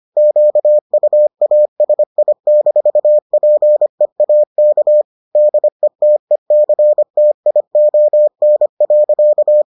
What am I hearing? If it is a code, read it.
Morse code, heard as a keyed tone, about 25 wpm, 600 Hz. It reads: QUASI-PEAK DETECTION.